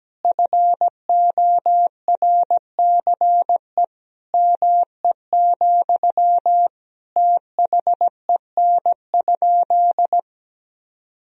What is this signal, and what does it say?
Morse code, heard as a keyed tone, about 17 wpm, 700 Hz. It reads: FORCE ME, THEN?
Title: Sentences from Top 1000 Words - Review Part 3 - 17wpm